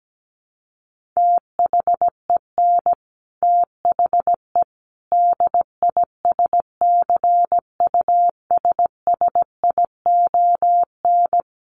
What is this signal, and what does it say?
Morse code, heard as a keyed tone, about 17 wpm, 700 Hz. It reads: THEN THE DISCUSSION
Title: Sentences from Top 1100 Words 17wpm